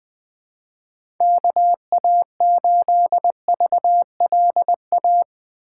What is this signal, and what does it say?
Morse code, heard as a keyed tone, about 20 wpm, 700 Hz. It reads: KA8VLA
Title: USA Call Signs - 6x Character Spacing - 20wpm